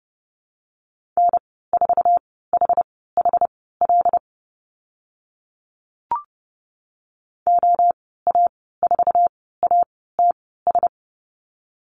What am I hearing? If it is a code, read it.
Morse code, heard as a keyed tone, about 30 wpm, spaced as for 17 wpm, 700 Hz. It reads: N4HHL OA4ATS
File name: Lesson 14g - Number 2 - Call Signs 30wpm